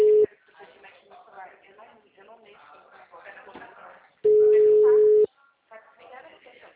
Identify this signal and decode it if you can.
Ringback tone (Europe, 425 Hz), with speech.